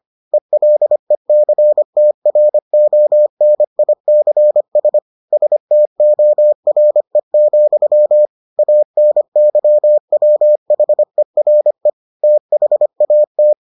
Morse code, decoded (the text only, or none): ELECTRONICS STORE, ANYWHERE THAT